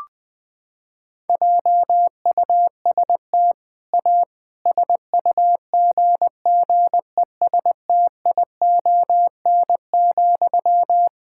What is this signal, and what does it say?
Morse code, heard as a keyed tone, about 20 wpm, 700 Hz. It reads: JUST A SUGGESTION,
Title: Sentences from Top 1600 Words 20wpm